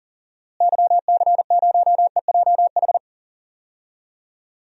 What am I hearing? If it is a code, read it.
Morse code, heard as a keyed tone, about 40 wpm, 700 Hz. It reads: YC0EJH
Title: DXCC-100 Call Signs - 6 Characters - Repeated 3x - 40wpm